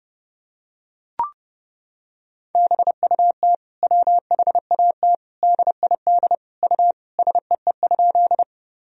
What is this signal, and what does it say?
Morse code, heard as a keyed tone, about 32 wpm, 700 Hz. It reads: BUT WHAT DID U SEE?
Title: Sentences from Top 100 Words and Abbreviations 30wpm